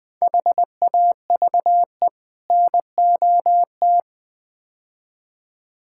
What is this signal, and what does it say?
Morse code, heard as a keyed tone, about 20 wpm, 700 Hz. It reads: HAVE NOT
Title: Build A Phrase - 5-Word Phrases - 20wpm